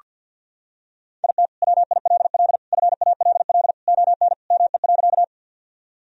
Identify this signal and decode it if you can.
Morse code, heard as a keyed tone, about 48 wpm, 700 Hz. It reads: IT WILL FALL ON ME.
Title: Sentences from Top 1000 Words - Review Part 2 - 50wpm